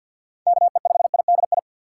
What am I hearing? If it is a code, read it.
Morse code, heard as a keyed tone, about 48 wpm, 700 Hz. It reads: KE5IDI
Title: USA Call Signs - 6 Characters - 6x Character Spacing - 50wpm